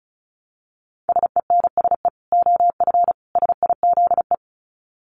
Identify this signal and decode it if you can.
Morse code, heard as a keyed tone, about 34 wpm, 700 Hz. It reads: SENSE OF SIZE
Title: Dit Challenge 35wpm